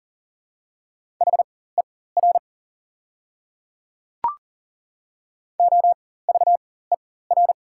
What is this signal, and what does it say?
Morse code, heard as a keyed tone, about 40 wpm, spaced as for 19 wpm, 700 Hz. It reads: HER OVER
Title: 4x Character Spacing - Top 100 Words - 40wpm